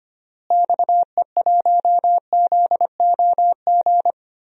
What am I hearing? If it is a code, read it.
Morse code, heard as a keyed tone, about 25 wpm, 700 Hz. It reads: XE1ZOG
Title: DXCC-100 Call Signs - 6 Characters - Repeated 3x - 25wpm